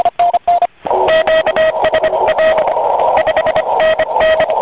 Morse code, heard as a keyed tone, about 25 wpm, 1900 Hz, with radio noise. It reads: QSA 5NN